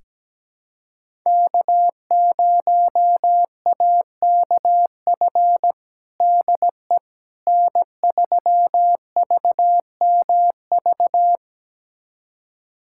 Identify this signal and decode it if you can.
Morse code, heard as a keyed tone, about 17 wpm, 700 Hz. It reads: K0AKF DE N3VMV
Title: USA Callsign DE Callsign - 5 Characters - 17wpm